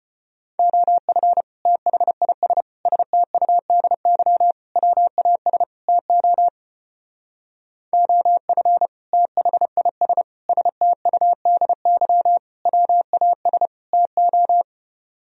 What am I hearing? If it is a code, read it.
Morse code, heard as a keyed tone, about 30 wpm, 700 Hz. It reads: OF THIS STUDY WAS TO OF THIS STUDY WAS TO
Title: Speed-Racing - Sets of 5 Words 25wpm